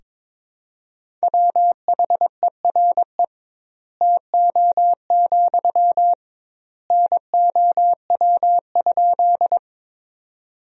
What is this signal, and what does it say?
Morse code, heard as a keyed tone, about 22 wpm, 700 Hz. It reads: WHERE TO, NOW?
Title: 2x Word-Spacing - Sentences from Top 200 Words - 22wpm